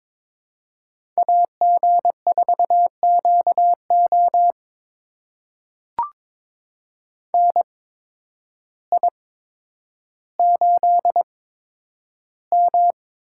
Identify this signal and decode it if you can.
Morse code, heard as a keyed tone, about 22 wpm, spaced as for 10 wpm, 700 Hz. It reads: AG4QO NI8M
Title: USA Call Signs - 8x Character Spacing - 22wpm